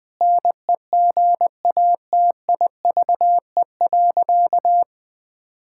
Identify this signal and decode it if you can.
Morse code, heard as a keyed tone, about 20 wpm, 700 Hz. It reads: NEGATIVE.
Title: Sentences from Top 1000 Words - Review Part 6 - 20wpm